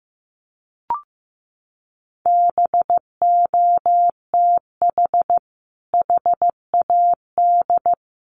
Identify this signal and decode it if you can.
Morse code, heard as a keyed tone, about 15 wpm, 700 Hz. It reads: BOTH HAD
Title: Sentences from Top 1200 Words 15wpm